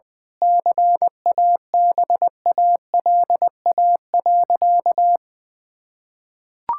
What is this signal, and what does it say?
Morse code, heard as a keyed tone, about 19 wpm, 700 Hz. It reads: CABALA.
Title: Pangram Challenge v2 - 20wpm